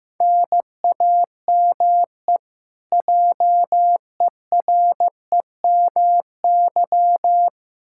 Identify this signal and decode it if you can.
Morse code, heard as a keyed tone, about 16 wpm, 700 Hz. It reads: NAME JEREMY
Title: K1USN SST QSO Practice 15wpm